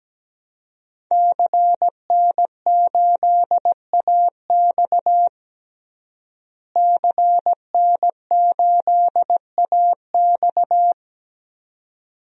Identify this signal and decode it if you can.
Morse code, heard as a keyed tone, about 17 wpm, 700 Hz. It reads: CN8AX CN8AX
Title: DXCC-100 Call Signs - 5 Characters - Repeated 2x - 17wpm